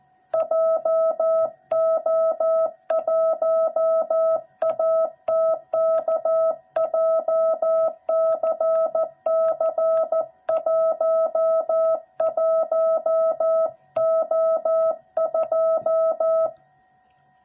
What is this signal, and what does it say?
Morse code, heard as a keyed tone, about 14 wpm, 660 Hz. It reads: JO1ATK JCC11O2
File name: jh8cbh / 2021cw / cw06b